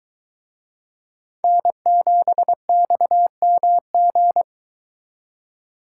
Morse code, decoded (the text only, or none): N7XMG